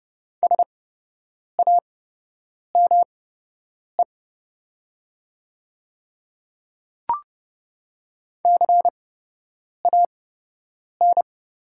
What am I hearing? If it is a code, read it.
Morse code, heard as a keyed tone, about 30 wpm, spaced as for 8 wpm, 700 Hz. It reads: SAME CAN
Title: 8x Character Spacing - Top 300 Words - 30wpm